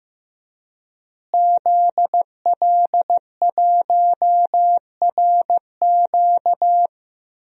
Morse code, heard as a keyed tone, about 15 wpm, 700 Hz. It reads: ZL1RQ
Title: DXCC-100 Call Signs - 5 Characters - Repeated 2x - 15wpm